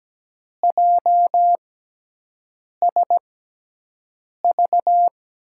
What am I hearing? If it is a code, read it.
Morse code, heard as a keyed tone, about 17 wpm, spaced as for 6 wpm, 700 Hz. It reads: JSV
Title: USA Call Signs - 6 Characters - 6x Character Spacing - 17wpm